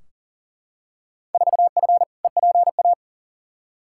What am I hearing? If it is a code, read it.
Morse code, heard as a keyed tone, about 37 wpm, 700 Hz. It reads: 4F EPA
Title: Field Day Exchanges - 40wpm